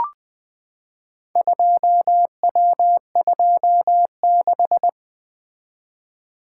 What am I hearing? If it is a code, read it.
Morse code, heard as a keyed tone, about 20 wpm, 700 Hz. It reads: 2W26
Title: Four Letter-Number Combinations - 20wpm